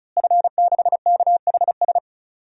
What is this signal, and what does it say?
Morse code, heard as a keyed tone, about 36 wpm, 700 Hz. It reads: F6KHS
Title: DXCC-100 Call Signs - 5 Characters - Repeated 2x - 35wpm